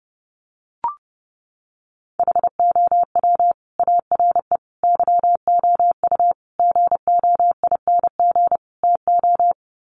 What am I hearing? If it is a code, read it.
Morse code, heard as a keyed tone, about 30 wpm, 700 Hz. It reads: HOW ARE YOU GOING TO
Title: Sets of 5 Words 30wpm